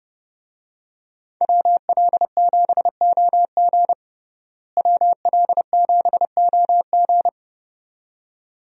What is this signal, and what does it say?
Morse code, heard as a keyed tone, about 32 wpm, 700 Hz. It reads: WL7OG WL7OG
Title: USA Call Signs - 5 Characters - Repeated 2x - 30wpm